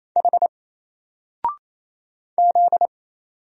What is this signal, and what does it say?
Morse code, heard as a keyed tone, about 27 wpm, 700 Hz. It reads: H Z
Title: Single Letter-Number Alt 28wpm